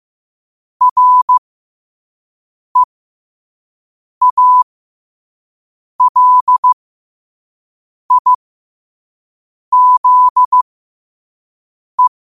Morse code, heard as a keyed tone, about 15 wpm, spaced as for 5 wpm, 1000 Hz. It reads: REALIZE
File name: qotdfiles / 5wpm